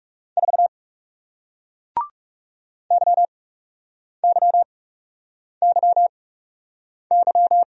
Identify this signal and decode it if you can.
Morse code, heard as a keyed tone, about 38 wpm, spaced as for 8 wpm, 700 Hz. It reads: 4 YYYY